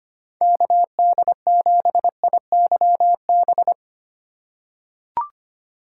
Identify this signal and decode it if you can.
Morse code, heard as a keyed tone, about 25 wpm, 700 Hz. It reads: KD7IYB
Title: USA Call Signs - 6 Characters - 2x Character Spacing - 25wpm